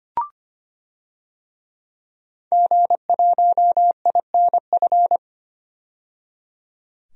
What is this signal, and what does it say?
Morse code, heard as a keyed tone, about 25 wpm, 700 Hz. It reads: G1INF
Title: DXCC-100 Call Signs - 5 Characters - 25wpm